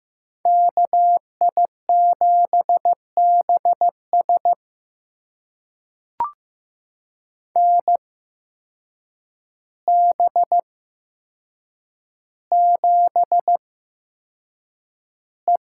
Morse code, heard as a keyed tone, about 15 wpm, spaced as for 7 wpm, 700 Hz. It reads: KI7BS NB7E
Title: USA Call Signs - 8x Character Spacing - 15wpm